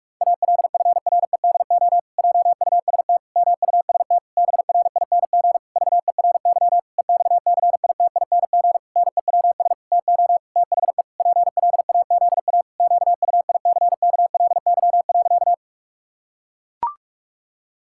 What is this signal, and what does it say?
Morse code, heard as a keyed tone, about 44 wpm, 700 Hz. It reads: ALFREDO JUST MUST BRING VERY EXCITING NEWS TO THE PLAZA QUICKLY.
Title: Pangram Challenge v2 - 45wpm